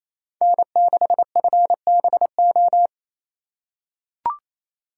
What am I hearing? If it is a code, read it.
Morse code, heard as a keyed tone, about 28 wpm, 700 Hz. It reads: N6FBO
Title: USA Call Signs - 4x Character Spacing - 28wpm